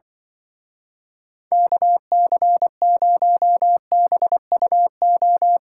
Morse code, heard as a keyed tone, about 24 wpm, 700 Hz. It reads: KC0BUO
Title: Speed Racing - USA 2x3 Call Signs 20wpm